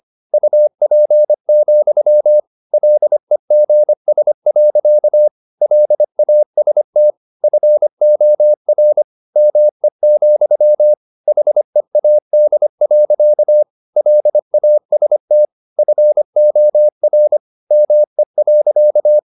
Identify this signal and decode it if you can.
Morse code, heard as a keyed tone, about 25 wpm, 600 Hz. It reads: UP, LEGS. LAST FOR ME, HEAD. LAST FOR ME.